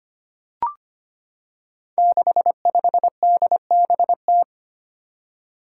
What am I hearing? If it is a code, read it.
Morse code, heard as a keyed tone, about 25 wpm, 700 Hz. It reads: 65DBT
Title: Five Challenging Letter-Number Combinations 25wpm